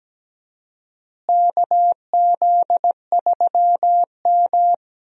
Morse code, heard as a keyed tone, about 17 wpm, 700 Hz. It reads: KZ3M
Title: USA Call Signs - 4 Characters - Repeated 2x - 17wpm